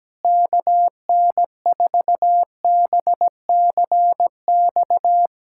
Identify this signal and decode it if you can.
Morse code, heard as a keyed tone, about 17 wpm, 700 Hz. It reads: KN4BCX
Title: USA Callsign DE Callsign 17wpm